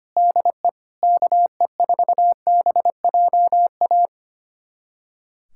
Morse code, heard as a keyed tone, about 25 wpm, 700 Hz. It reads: DE KE4BJA
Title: USA Callsign DE Callsign - 6 Characters - 25wpm